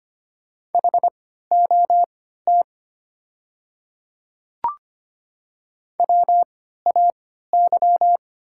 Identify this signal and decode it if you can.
Morse code, heard as a keyed tone, about 25 wpm, spaced as for 14 wpm, 700 Hz. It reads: HOT WAY
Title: 3x Character Spacing - Top 100 Words - 25wpm